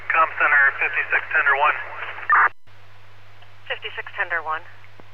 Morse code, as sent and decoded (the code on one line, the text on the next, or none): none